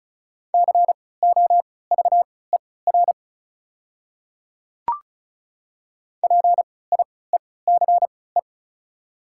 Morse code, heard as a keyed tone, about 34 wpm, spaced as for 20 wpm, 700 Hz. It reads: COVER PIECE
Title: 3x Character Spacing - Top 500 Words - 35wpm